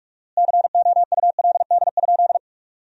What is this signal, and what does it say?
Morse code, heard as a keyed tone, about 41 wpm, 700 Hz. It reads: COULD?
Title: 4x Word-Spacing - Sentences from Top 400 Words - 45wpm